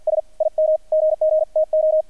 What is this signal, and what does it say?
Morse code, heard as a keyed tone, about 48 wpm, 630 Hz, with radio noise. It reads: S IM GGTO
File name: archived / CRSnet